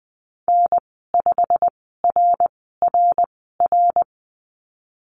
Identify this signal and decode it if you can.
Morse code, heard as a keyed tone, about 20 wpm, spaced as for 15 wpm, 700 Hz. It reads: N5RRR
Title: 173 Lesson 28g - Number 9 - Call Signs 20wpm